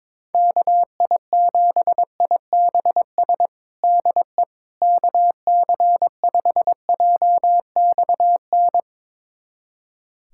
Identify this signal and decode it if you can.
Morse code, heard as a keyed tone, about 22 wpm, 700 Hz. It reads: KI7IBS DE KC5JXN
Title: USA Callsign DE Callsign 22wpm